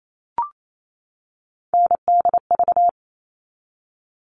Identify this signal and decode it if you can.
Morse code, heard as a keyed tone, about 28 wpm, 700 Hz. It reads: NDV
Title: All Cut Numbers - 3-Digit 28wpm